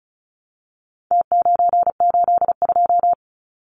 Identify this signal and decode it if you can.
Morse code, heard as a keyed tone, about 34 wpm, 700 Hz. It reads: T982
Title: Numbers - Contest Style - 4-Digit 35wpm